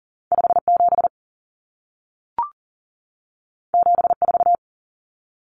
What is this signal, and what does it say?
Morse code, heard as a keyed tone, about 40 wpm, 700 Hz. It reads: 57 74
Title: Common Cut Numbers - 2-Digit 40wpm